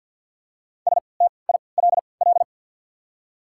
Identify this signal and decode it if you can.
Morse code, heard as a keyed tone, about 54 wpm, spaced as for 29 wpm, 690 Hz. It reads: STILL
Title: 3x Character Spacing - Top 500 Words - 50wpm